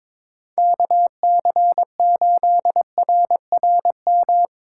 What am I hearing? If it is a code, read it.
Morse code, heard as a keyed tone, about 22 wpm, 700 Hz. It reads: KC8RRM
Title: USA Call Signs - 6 Characters - 4x Character Spacing - 22wpm